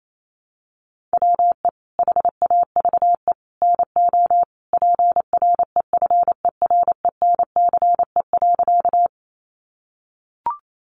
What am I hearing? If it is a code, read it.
Morse code, heard as a keyed tone, about 28 wpm, 700 Hz. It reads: WE HAVE NO PREFERENCE.
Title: Sentences from Top 1500 Words 28wpm